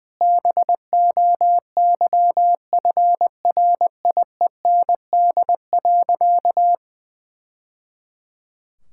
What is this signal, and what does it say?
Morse code, heard as a keyed tone, about 20 wpm, 700 Hz. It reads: BOYFRIEND.
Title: Sentences from Top 1000 Words - Review Part 6 - 20wpm